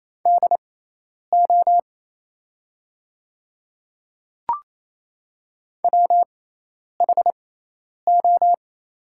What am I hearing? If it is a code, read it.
Morse code, heard as a keyed tone, about 28 wpm, spaced as for 10 wpm, 700 Hz. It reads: DO WHO
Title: 6x Character Spacing - Top 100 Words - 28wpm